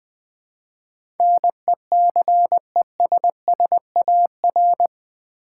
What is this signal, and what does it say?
Morse code, heard as a keyed tone, about 20 wpm, 700 Hz. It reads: NECESSAR
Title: Build A Word - 4 Letters At A Time - 11-Letter Words - 20wpm